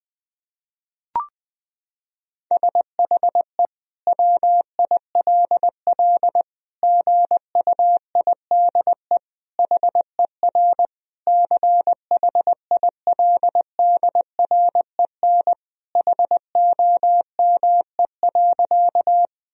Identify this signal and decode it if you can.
Morse code, heard as a keyed tone, about 20 wpm, 700 Hz. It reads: SHE WILL GUIDE HER CHILDREN HOME.
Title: Sentences from Top 1000 Words - Review Part 3 - 20wpm